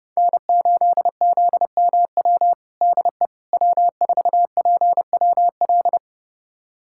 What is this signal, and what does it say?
Morse code, heard as a keyed tone, about 30 wpm, 700 Hz. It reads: N8ZMW DE W4PWL
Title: USA Callsign DE Callsign - 5 Characters - 30wpm